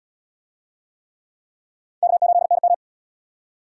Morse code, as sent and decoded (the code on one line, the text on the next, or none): -.- ---.. -. --
K8NM